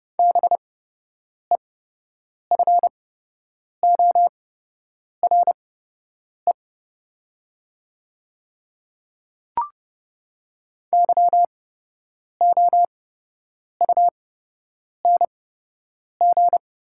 Morse code, heard as a keyed tone, about 30 wpm, spaced as for 8 wpm, 690 Hz. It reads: BEFORE YOUNG
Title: 8x Character Spacing - Top 500 Words - 30wpm